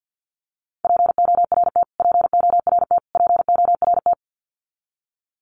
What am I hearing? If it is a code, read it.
Morse code, heard as a keyed tone, about 50 wpm, 700 Hz. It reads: PORT PORT PORT
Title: Top 500 Words Repeated 50wpm